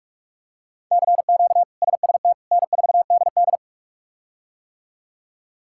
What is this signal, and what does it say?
Morse code, heard as a keyed tone, about 43 wpm, 700 Hz. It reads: CQ SST N4DD
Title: K1USN SST QSO Practice - 45wpm